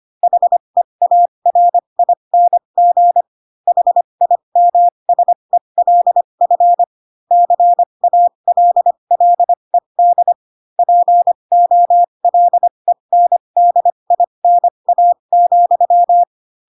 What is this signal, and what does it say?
Morse code, heard as a keyed tone, about 25 wpm, spaced as for 22 wpm, 700 Hz. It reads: HEARING HIMSELF CALLED POLENDINA,
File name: Pinocchio / Pinocchio ch2 18 25 ik1hge